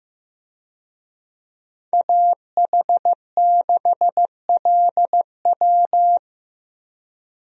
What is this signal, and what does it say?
Morse code, heard as a keyed tone, about 15 wpm, 700 Hz. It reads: AH6LW
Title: USA Call Signs - 5 Characters - Repeated 2x - 15wpm